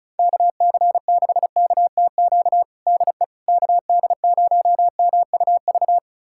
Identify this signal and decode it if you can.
Morse code, heard as a keyed tone, about 36 wpm, 700 Hz. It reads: KC6KTQ DE KD0MUV
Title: USA Callsign DE Callsign 35wpm